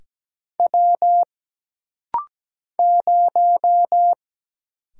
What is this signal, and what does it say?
Morse code, heard as a keyed tone, about 17 wpm, 700 Hz. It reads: W 0
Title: Single Letter-Number Alt - Rapid-Fire 17wpm